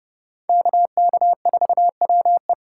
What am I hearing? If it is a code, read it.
Morse code, heard as a keyed tone, about 36 wpm, 700 Hz. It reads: KK4WE
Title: Speed Racing - USA 2x2 Call Signs 30wpm